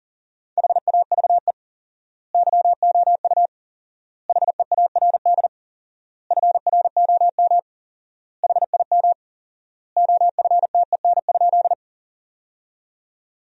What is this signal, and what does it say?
Morse code, heard as a keyed tone, about 40 wpm, 700 Hz. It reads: HAVE YOU HEARD FROM HIM OFTEN?
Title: 4x Word-Spacing - Sentences from Top 400 Words - 40wpm